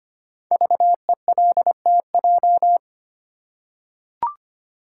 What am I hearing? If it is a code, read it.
Morse code, heard as a keyed tone, about 25 wpm, 700 Hz. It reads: VELTJ
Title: Code Groups - 5 Letter:Numbers - 25wpm